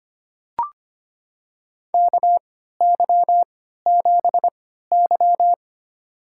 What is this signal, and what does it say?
Morse code, heard as a keyed tone, about 25 wpm, spaced as for 14 wpm, 700 Hz. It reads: KY7Y
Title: USA Call Signs - 4 Characters - 3x Character Spacing - 25wpm